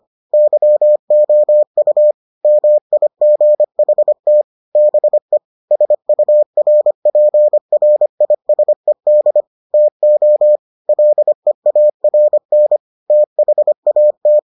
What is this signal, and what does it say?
Morse code, heard as a keyed tone, about 25 wpm, 600 Hz. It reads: YOU MIGHT BE SURPRISED TO LEARN THAT